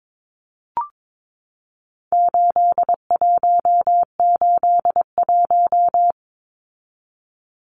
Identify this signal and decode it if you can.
Morse code, heard as a keyed tone, about 22 wpm, 700 Hz. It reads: 8181
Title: Numbers - 4-Digit 22wpm